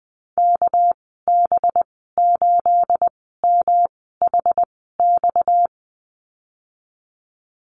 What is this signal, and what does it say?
Morse code, heard as a keyed tone, about 19 wpm, 700 Hz. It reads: K B 8 M H X
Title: 238 Lesson 38g - Letter q - Call Signs 20wpm